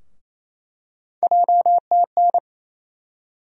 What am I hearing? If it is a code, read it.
Morse code, heard as a keyed tone, about 28 wpm, 700 Hz. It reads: JTN
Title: USA Call Sign Postfixes - 3 Letters 28wpm